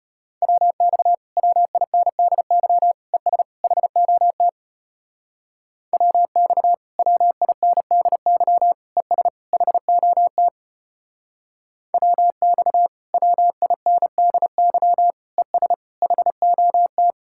Morse code, heard as a keyed tone, about 33 wpm, 700 Hz. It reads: WX WINDY ES HOT WX WINDY ES HOT WX WINDY ES HOT